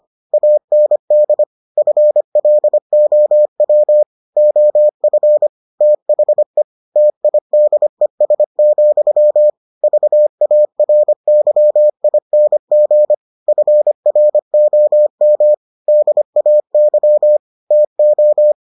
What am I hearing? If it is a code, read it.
Morse code, heard as a keyed tone, about 25 wpm, 600 Hz. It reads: AND FLOW OF THE TIDES, VARYING FROM DAY TO